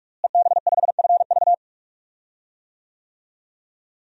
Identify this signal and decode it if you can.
Morse code, heard as a keyed tone, about 46 wpm, 700 Hz. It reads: EB5FV
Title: DXCC-100 Call Signs - 5 Characters - Repeated 3x - 45wpm